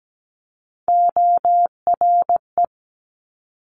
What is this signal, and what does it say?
Morse code, heard as a keyed tone, about 17 wpm, 700 Hz. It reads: ORE
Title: Multi-letter Phonemes Quick Fire 17wpm